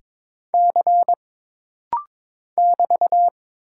Morse code, heard as a keyed tone, about 22 wpm, 700 Hz. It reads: C <BT>
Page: This rapid-fire timing is geared to push your ICR (Instant Character Recognition) skills.